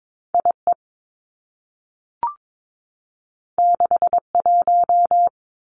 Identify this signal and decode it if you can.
Morse code, heard as a keyed tone, about 22 wpm, 700 Hz. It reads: I E 6 1
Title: Two Challenging Letter-Number Combinations 22wpm